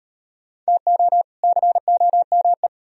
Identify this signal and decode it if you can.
Morse code, heard as a keyed tone, about 38 wpm, 700 Hz. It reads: TO COME